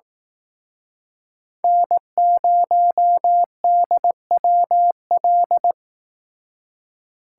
Morse code, heard as a keyed tone, about 19 wpm, 700 Hz. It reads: N0DWL
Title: Speed Racing - USA 1x3 Call Signs 15wpm